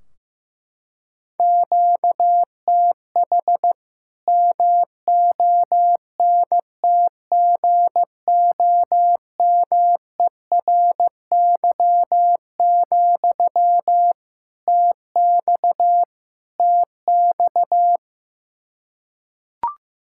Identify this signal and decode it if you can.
Morse code, heard as a keyed tone, about 15 wpm, 700 Hz. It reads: QTH MONTGOMERY, TX TX
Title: K1USN SST QSO Practice 15wpm